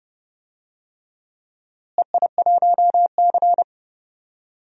Morse code, heard as a keyed tone, about 30 wpm, 700 Hz. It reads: EI1C
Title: DXCC-100 Call Signs - 4 Characters - Repeated 3x - 30wpm